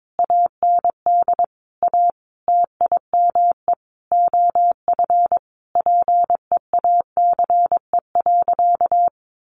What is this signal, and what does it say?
Morse code, heard as a keyed tone, about 22 wpm, 700 Hz. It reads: AND A TIME OF PEACE.
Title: Sentences from Top 1500 Words 22wpm